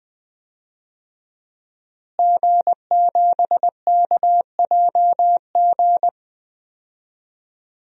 Morse code, heard as a keyed tone, about 20 wpm, 700 Hz. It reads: G7KJG
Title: DXCC-100 Call Signs - 5 Characters - Repeated 3x - 20wpm